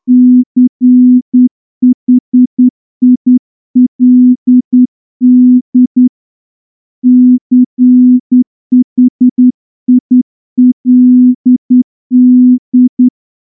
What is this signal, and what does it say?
Morse code, heard as a keyed tone, about 10 wpm, 250 Hz. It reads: CHILD CHILD
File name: morse-training-10WPM